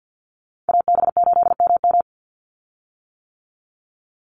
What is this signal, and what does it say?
Morse code, heard as a keyed tone, about 50 wpm, 700 Hz. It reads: AB8MM
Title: USA Call Signs 50wpm